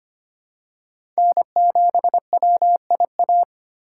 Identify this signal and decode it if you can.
Morse code, heard as a keyed tone, about 25 wpm, 700 Hz. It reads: N7WIA
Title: USA Call Signs - 5 Characters - 2x Character Spacing - 25wpm